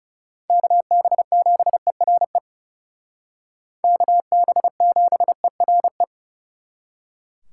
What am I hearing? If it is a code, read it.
Morse code, heard as a keyed tone, about 36 wpm, 700 Hz. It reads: KB7ERE KB7ERE
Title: Speed Racing - USA 2x3 Call Signs 30wpm